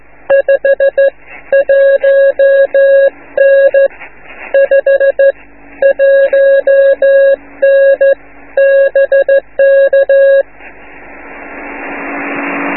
Morse code, heard as a keyed tone, about 14 wpm, 550 Hz. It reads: 5 1N 5 1N BK